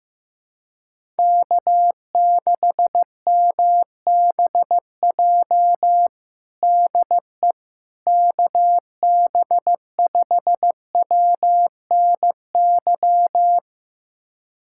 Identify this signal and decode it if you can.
Morse code, heard as a keyed tone, about 16 wpm, 700 Hz. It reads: K6MBJ DE KB5WNY
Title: USA Callsign DE Callsign 15wpm